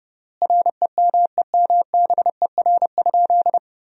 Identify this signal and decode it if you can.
Morse code, heard as a keyed tone, about 30 wpm, 700 Hz. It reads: REMEMBER?
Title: 3x Word-Spacing - Sentences from Top 400 Words - 30wpm